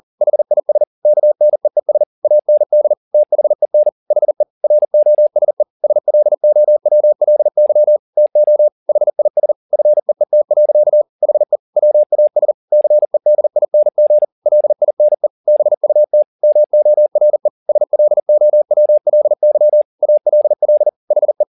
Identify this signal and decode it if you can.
Morse code, heard as a keyed tone, about 40 wpm, 600 Hz. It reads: HIS KNEES AND THEN HE ROSE SLOWLY TO HIS FEET. HE WAS CEDING LINE BUT MORE SLOWLY ALL HE
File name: the old man and the sea 2 40wpm